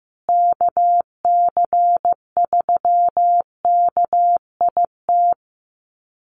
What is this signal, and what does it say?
Morse code, heard as a keyed tone, about 15 wpm, 700 Hz. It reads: KC3KIT
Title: USA Call Signs Repeated 3x 15wpm